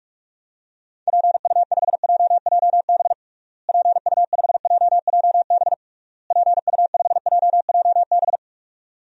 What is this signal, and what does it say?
Morse code, heard as a keyed tone, about 41 wpm, 700 Hz. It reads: PU5JJB PU5JJB PU5JJB
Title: DXCC-100 Call Signs - 6 Characters - Repeated 3x - 45wpm